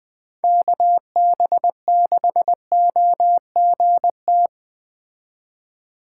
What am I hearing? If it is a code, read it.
Morse code, heard as a keyed tone, about 20 wpm, 700 Hz. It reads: KB6OGT
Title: Speed Racing - USA 2x3 Call Signs 17wpm